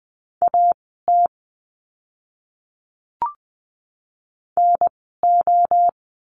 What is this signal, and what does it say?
Morse code, heard as a keyed tone, about 20 wpm, spaced as for 14 wpm, 700 Hz. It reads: AT NO
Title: 030 Lesson 6f - Letter i - Words 20wpm